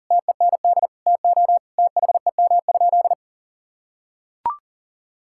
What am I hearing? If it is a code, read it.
Morse code, heard as a keyed tone, about 40 wpm, 700 Hz. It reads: TEND TO THEM?
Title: Sentences from Top 1000 Words - Review Part 5 - 40wpm